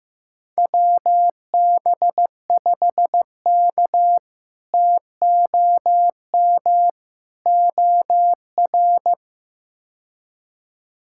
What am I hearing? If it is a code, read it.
Morse code, heard as a keyed tone, about 15 wpm, 700 Hz. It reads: WB5K TOM OR
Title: K1USN SST QSO Practice - 15wpm